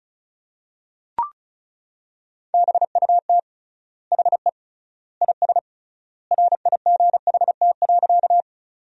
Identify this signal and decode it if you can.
Morse code, heard as a keyed tone, about 36 wpm, 700 Hz. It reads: BUT HE IS RIGHT.
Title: 3x Word-Spacing - Sentences from Top 200 Words - 35wpm